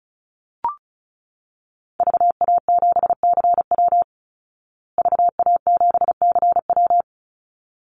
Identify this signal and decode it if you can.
Morse code, heard as a keyed tone, about 34 wpm, 700 Hz. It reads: VA7CW VA7CW
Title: Call Signs Repeated 3x - 35wpm